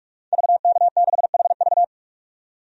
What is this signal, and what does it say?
Morse code, heard as a keyed tone, about 46 wpm, 700 Hz. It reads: VK6HV
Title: DXCC-100 Call Signs - 5 Characters - 45wpm